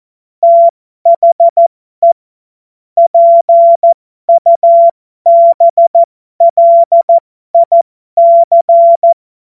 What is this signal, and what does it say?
Morse code, heard as a keyed tone, about 14 wpm, spaced as for 12 wpm, 680 Hz. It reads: THE PUBLIC
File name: Morsekurs / Witze